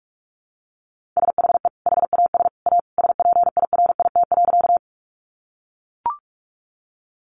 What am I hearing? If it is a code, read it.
Morse code, heard as a keyed tone, about 44 wpm, 700 Hz. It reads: SHE HAS A SPIRIT.
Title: Sentences from Top 1400 Words 45wpm